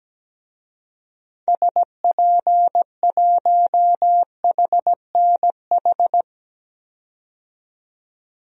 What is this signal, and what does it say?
Morse code, heard as a keyed tone, about 17 wpm, 700 Hz. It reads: SP1HNH
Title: DXCC-100 Call Signs - 6 Characters - 17wpm